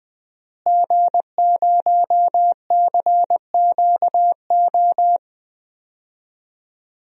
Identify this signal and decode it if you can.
Morse code, heard as a keyed tone, about 20 wpm, 700 Hz. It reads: G0CQO
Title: DXCC-100 Call Signs - 5 Characters - Repeated 2x - 20wpm